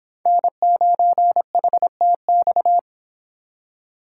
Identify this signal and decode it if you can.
Morse code, heard as a keyed tone, about 26 wpm, 700 Hz. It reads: N9HTX